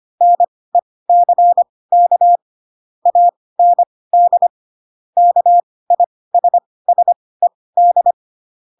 Morse code, heard as a keyed tone, about 25 wpm, spaced as for 18 wpm, 700 Hz. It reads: NECK AND KISSED